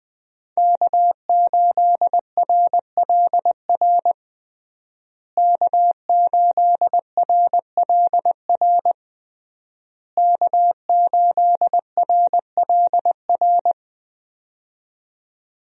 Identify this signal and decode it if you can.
Morse code, heard as a keyed tone, about 21 wpm, 700 Hz. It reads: K8RLR K8RLR K8RLR
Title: USA Call Signs - 5 Characters - Repeated 3x - 20wpm